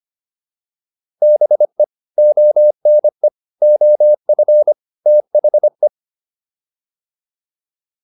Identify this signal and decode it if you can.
Morse code, heard as a keyed tone, about 25 wpm, 600 Hz. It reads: BE ONE OF THE
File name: WD304 25